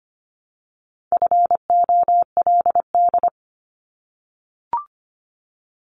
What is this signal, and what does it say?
Morse code, heard as a keyed tone, about 25 wpm, 700 Hz. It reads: FOLD
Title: Multi-letter Phonemes 25wpm